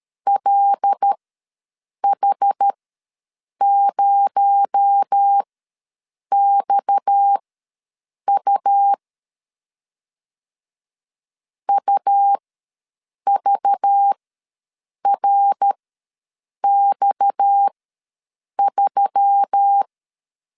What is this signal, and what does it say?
Morse code, heard as a keyed tone, about 13 wpm, spaced as for 7 wpm, 800 Hz. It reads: LH0XU UVRX3
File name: LEZIONI / 6 PKX0 / 6 PKX0 ws 12wpm 3448